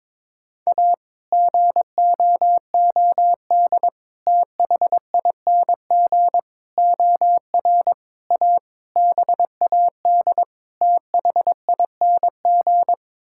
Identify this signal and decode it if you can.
Morse code, heard as a keyed tone, about 22 wpm, 700 Hz. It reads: A GOOD THING OR A BAD THING
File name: practice / 22wpm / Sets of 7 Words - 22wpm